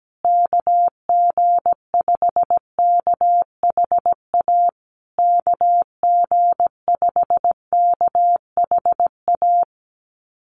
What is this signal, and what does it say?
Morse code, heard as a keyed tone, about 17 wpm, 700 Hz. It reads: KG5KHA KG5KHA
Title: USA Call Signs Repeated 3x 17wpm